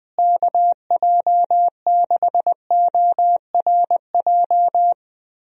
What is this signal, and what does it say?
Morse code, heard as a keyed tone, about 20 wpm, 700 Hz. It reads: KJ6ORJ
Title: USA Call Signs - 6 Characters - Repeated 2x - 20wpm